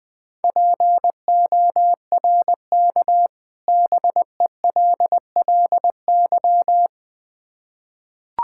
Morse code, heard as a keyed tone, about 20 wpm, 700 Hz. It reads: PORK BELLY
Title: Difficult Words with Hints - 20wpm